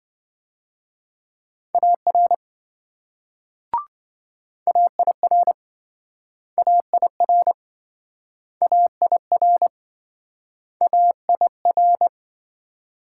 Morse code, heard as a keyed tone, about 26 wpm, 700 Hz. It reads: AR AIR AIR AIR AIR